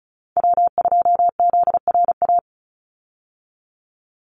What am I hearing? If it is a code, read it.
Morse code, heard as a keyed tone, about 34 wpm, 700 Hz. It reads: W2ZRA
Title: Call Signs Repeated 2x - 35wpm